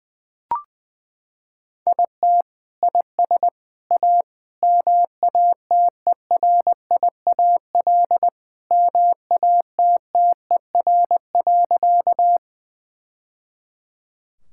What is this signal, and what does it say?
Morse code, heard as a keyed tone, about 20 wpm, 700 Hz. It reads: IT IS A MATERIAL MATTER.
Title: Sentences from Top 1000 Words - Review Part 5 - 20wpm